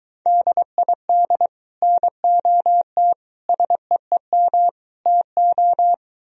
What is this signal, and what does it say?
Morse code, heard as a keyed tone, about 23 wpm, 700 Hz. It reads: DID NOT SEEM TO